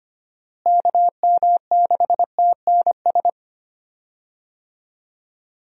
Morse code, heard as a keyed tone, about 25 wpm, 700 Hz. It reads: KM6TNS
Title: USA Call Signs - 6 Characters - Repeated 2x - 25wpm